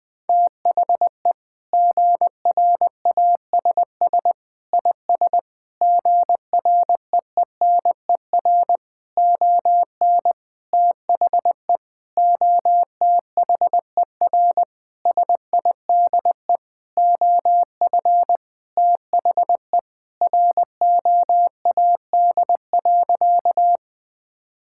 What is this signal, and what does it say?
Morse code, heard as a keyed tone, about 20 wpm, 700 Hz. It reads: THE GRASS IS GREENER ON THE OTHER SIDE OF THE ROAD.
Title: Sentences with 4-Word Phrases - Spoken Beginning - 20wpm